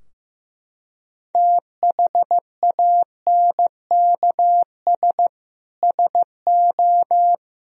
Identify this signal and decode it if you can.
Morse code, heard as a keyed tone, about 15 wpm, 700 Hz. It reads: THANKS SO
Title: Sentences from Top 1000 Words - Review Part 3 - 15wpm